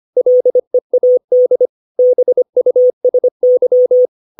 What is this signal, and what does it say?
Morse code, heard as a keyed tone, about 25 wpm, 500 Hz. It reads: LEAD BUSY
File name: 632 words / 632 Words 25WPM